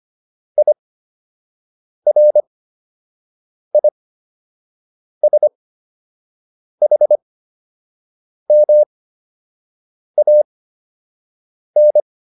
Morse code, heard as a keyed tone, about 25 wpm, spaced as for 6 wpm, 600 Hz. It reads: IRISHMAN